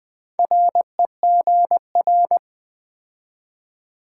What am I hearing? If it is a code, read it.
Morse code, heard as a keyed tone, about 20 wpm, 700 Hz. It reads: REGR
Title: Build A Word - 2 Letters At A Time - 6-Letter Words - 20wpm